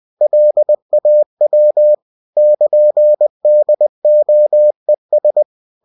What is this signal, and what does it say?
Morse code, heard as a keyed tone, about 20 wpm, 600 Hz. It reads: LAW <KN>DOES
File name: cw-books / winnie-the-pooh / winnie-20wpm-page-0115